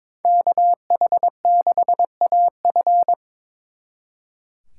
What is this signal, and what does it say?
Morse code, heard as a keyed tone, about 25 wpm, 700 Hz. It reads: KH6AF
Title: Speed Racing - USA 2x2 Call Signs 22wpm